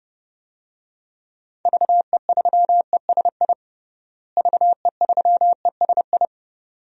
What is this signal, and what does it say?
Morse code, heard as a keyed tone, about 29 wpm, 700 Hz. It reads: VE3ESI VE3ESI
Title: DXCC-100 Call Signs - 6 Characters - Repeated 2x - 30wpm